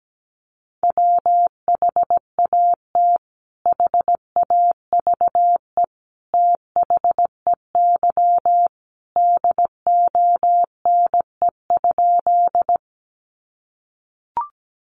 Morse code, heard as a keyed tone, about 17 wpm, 700 Hz. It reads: WHAT HAVE THEY DONE?
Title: Sentences from Top 500 Words 17wpm